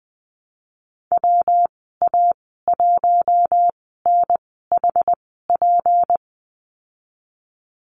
Morse code, heard as a keyed tone, about 20 wpm, spaced as for 15 wpm, 700 Hz. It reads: WA1NHP
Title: 173 Lesson 28g - Number 9 - Call Signs 20wpm